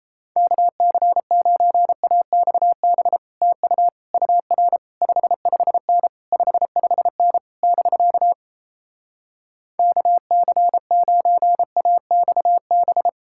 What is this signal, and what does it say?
Morse code, heard as a keyed tone, about 37 wpm, 700 Hz. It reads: KC9AXB TU UR 55N 55N <BK> KC9AXB